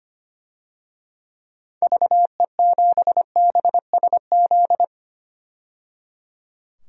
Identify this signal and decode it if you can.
Morse code, heard as a keyed tone, about 25 wpm, 700 Hz. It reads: VE7BSZ
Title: DXCC-100 Call Signs - 6 Characters - 25wpm